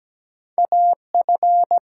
Morse code, heard as a keyed tone, about 17 wpm, 700 Hz. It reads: AF
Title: USA Call Signs - 4 Characters - 8x Character Spacing - 17wpm